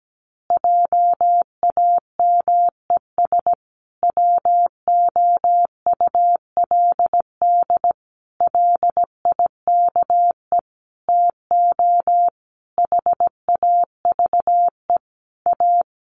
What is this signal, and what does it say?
Morse code, heard as a keyed tone, about 17 wpm, 700 Hz. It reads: JAMES WOULD LIKE TO HAVE A
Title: Easy Phrases 17wpm